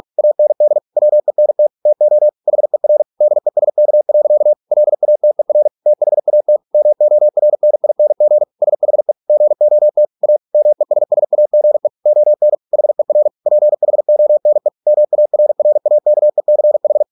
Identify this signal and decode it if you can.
Morse code, heard as a keyed tone, about 46 wpm, 600 Hz. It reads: AND WENT TO HER DESK. LATER THAT MORNING SHE GOT A MESSAGE ON HER PHONE MARRAKECH